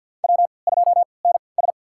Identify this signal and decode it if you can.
Morse code, heard as a keyed tone, about 49 wpm, spaced as for 28 wpm, 700 Hz. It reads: W2NS
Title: USA Call Signs - 4 Characters - 3x Character Spacing - 50wpm